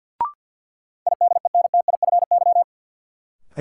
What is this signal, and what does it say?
Morse code, heard as a keyed tone, about 53 wpm, 700 Hz. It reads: IDENTIFY
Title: Sentences from Top 900 Words - New Word Drill - 50wpm